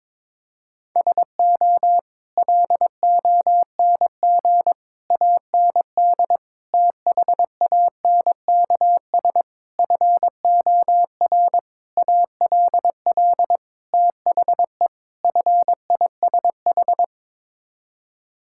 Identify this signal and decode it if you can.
Morse code, heard as a keyed tone, about 22 wpm, 700 Hz. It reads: SO LONG AND THANKS FOR ALL THE FISH
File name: practice / 22wpm / Sets of 8 Words - 22wpm